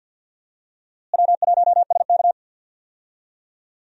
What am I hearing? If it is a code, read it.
Morse code, heard as a keyed tone, about 46 wpm, 700 Hz. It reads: W1SK
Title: USA Call Signs - 4 Characters - Repeated 3x - 50wpm